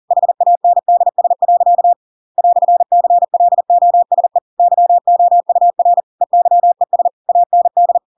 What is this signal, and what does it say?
Morse code, heard as a keyed tone, about 40 wpm, 700 Hz. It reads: HANDS. "CLOSE YOUR EYES AND
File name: Pinocchio / Pinocchio ch8 40 40 ik1hge